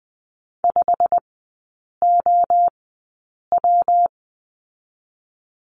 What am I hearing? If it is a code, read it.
Morse code, heard as a keyed tone, about 20 wpm, spaced as for 8 wpm, 700 Hz. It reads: 5OW
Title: 138 Lesson 23e - Letter f - Three Characters 20wpm